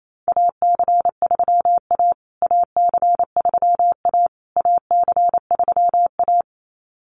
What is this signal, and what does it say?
Morse code, heard as a keyed tone, about 28 wpm, 700 Hz. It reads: AC3A AC3A AC3A
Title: USA Call Signs Repeated 3x 28wpm